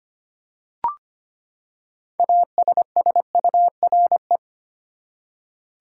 Morse code, heard as a keyed tone, about 25 wpm, 700 Hz. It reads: ASSURE
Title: Six Letter Words - 25wpm